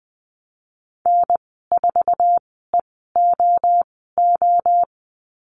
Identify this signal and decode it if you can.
Morse code, heard as a keyed tone, about 20 wpm, spaced as for 14 wpm, 700 Hz. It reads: N4EOO
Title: 048 Lesson 9g - Number 4 - Call Signs 20wpm